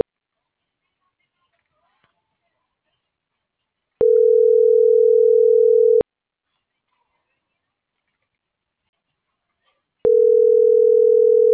Ringback tone (North America), 440 + 480 Hz.
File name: CENTRAL SAGOT / out miriam.mercado